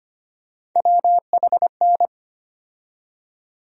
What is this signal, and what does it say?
Morse code, heard as a keyed tone, about 25 wpm, 700 Hz. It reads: WHN
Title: USA Call Sign Postfixes - 3 Letters 25wpm